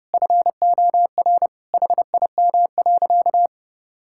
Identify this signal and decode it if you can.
Morse code, heard as a keyed tone, about 30 wpm, 700 Hz. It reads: FOR HIM.
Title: Sentences from Top 1000 Words - Review Part 6 - 30wpm